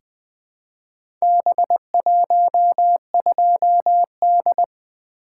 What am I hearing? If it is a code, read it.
Morse code, heard as a keyed tone, about 20 wpm, 700 Hz. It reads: B12D
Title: Four Letter-Number Combinations - 20wpm